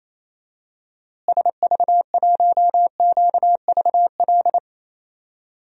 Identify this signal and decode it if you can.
Morse code, heard as a keyed tone, about 28 wpm, 700 Hz. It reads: SV1QVL
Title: DXCC-100 Call Signs - 6 Characters - Repeated 2x - 28wpm